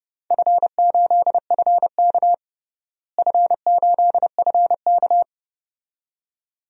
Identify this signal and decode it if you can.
Morse code, heard as a keyed tone, about 30 wpm, 700 Hz. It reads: F8FK F8FK
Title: DXCC-100 Call Signs - 4 Characters - Repeated 2x - 30wpm